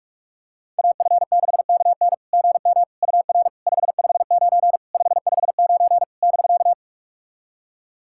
Morse code, heard as a keyed tone, about 46 wpm, 660 Hz. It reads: AF6KN GM UR 559 559 <BK>
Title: SOTA QSO Practice 45wpm